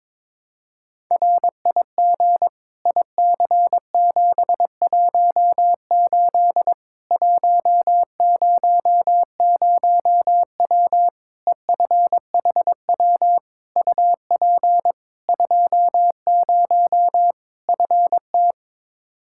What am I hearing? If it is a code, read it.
Morse code, heard as a keyed tone, about 22 wpm, 700 Hz. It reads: RIG IC718 100W EFHW UP 20 FT
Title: Practice QSO - Exchange 2 - Info - 22wpm